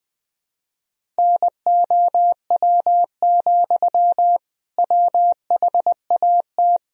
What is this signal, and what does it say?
Morse code, heard as a keyed tone, about 20 wpm, 700 Hz. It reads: NOW, WHAT
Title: Sentences from Top 200 Words and Abbreviations 20wpm